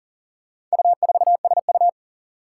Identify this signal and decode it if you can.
Morse code, heard as a keyed tone, about 40 wpm, 700 Hz. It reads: U4SU
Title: Four Challenging Letter-Number Combinations 40wpm